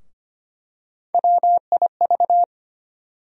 Morse code, heard as a keyed tone, about 25 wpm, 700 Hz. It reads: WIV
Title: USA Call Sign Postfixes - 3 Letters 25wpm